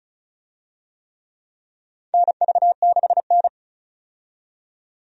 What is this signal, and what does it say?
Morse code, heard as a keyed tone, about 36 wpm, 700 Hz. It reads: NV6N
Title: USA Call Signs - 4 Characters - 35wpm